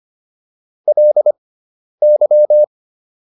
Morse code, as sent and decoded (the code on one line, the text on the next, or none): .-.. -.--
LY